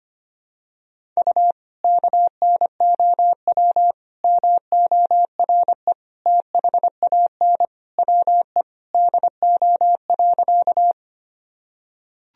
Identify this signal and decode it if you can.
Morse code, heard as a keyed tone, about 25 wpm, 700 Hz. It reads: U KNOW MORE THAN WE DO.
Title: Sentences from Top 100 Words and Abbreviations 25wpm